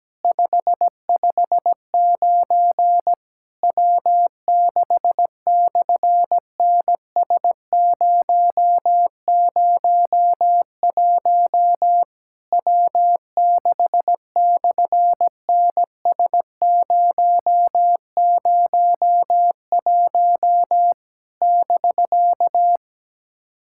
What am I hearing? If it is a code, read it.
Morse code, heard as a keyed tone, about 17 wpm, 700 Hz. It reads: 559 W6/NS001 W6/NS001 <BK>
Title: SOTA QSO Practice 17wpm